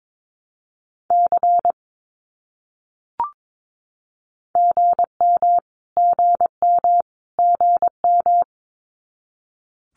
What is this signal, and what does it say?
Morse code, heard as a keyed tone, about 22 wpm, 700 Hz. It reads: C GM GM GM
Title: Abbreviations Repeated 3x - 22wpm